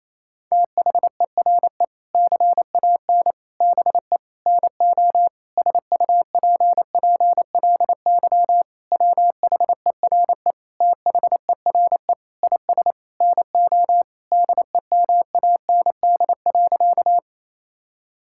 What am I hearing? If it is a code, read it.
Morse code, heard as a keyed tone, about 28 wpm, 700 Hz. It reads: THERE CAN BE NO SUPPLY WHERE THERE IS NO DEMAND.
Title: Sentences from Top 1000 Words - Review Part 5 - 28wpm